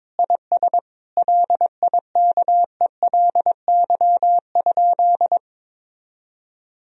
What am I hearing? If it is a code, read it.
Morse code, heard as a keyed tone, about 22 wpm, 700 Hz. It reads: IS LIKELY?
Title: Sentences from Top 1000 Words - Review Part 4 - 22wpm